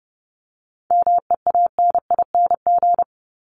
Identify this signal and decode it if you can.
Morse code, heard as a keyed tone, about 30 wpm, 700 Hz. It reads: MEANING
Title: ING Suffix - Top 100 Words with Base Words 30wpm